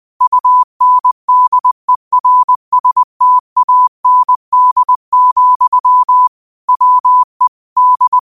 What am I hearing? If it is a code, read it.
Morse code, heard as a keyed tone, about 20 wpm, 1000 Hz. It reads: UNDERSTAND, WE D
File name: qotdfiles / 20wpm